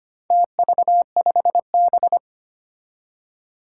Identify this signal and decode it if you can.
Morse code, heard as a keyed tone, about 25 wpm, 700 Hz. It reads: TV5B
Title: Four Challenging Letter-Number Combinations 25wpm